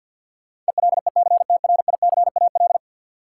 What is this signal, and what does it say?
Morse code, heard as a keyed tone, about 53 wpm, 700 Hz. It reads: ELECTRICAL
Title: Sentences from Top 1000 Words - Review Part 6 - New Word Drill - 50wpm